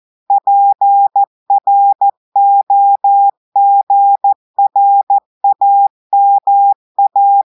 Morse code, heard as a keyed tone, about 14 wpm, 800 Hz. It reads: PROGRAMA
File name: audios / portugais / portugais texte en clair 2 vitesse 14 mots minutes